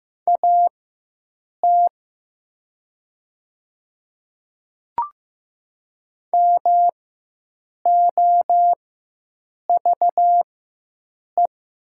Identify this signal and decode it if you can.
Morse code, heard as a keyed tone, about 15 wpm, spaced as for 7 wpm, 700 Hz. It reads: AT MOVE
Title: 4x Character Spacing - Top 200 Words - 15wpm